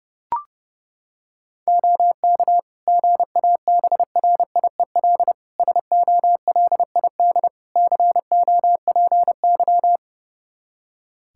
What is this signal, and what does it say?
Morse code, heard as a keyed tone, about 30 wpm, 700 Hz. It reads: OK GABRIEL SOLID COPY
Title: Practice QSO - Exchange 3 - More Info - 30wpm